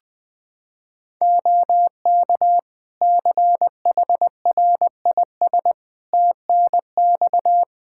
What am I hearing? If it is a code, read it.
Morse code, heard as a keyed tone, about 20 wpm, 700 Hz. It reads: OK CHRIS TNX
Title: Practice QSO - Exchange 4 - Ending - 20wpm